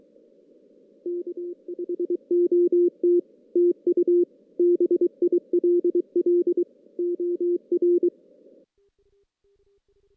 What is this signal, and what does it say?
Morse code, heard as a keyed tone, about 23 wpm, 350 Hz. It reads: K HOT TU BILL OR
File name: 2016 JAN CW NAQP / 2016 Jan naqp k5ot - 0-6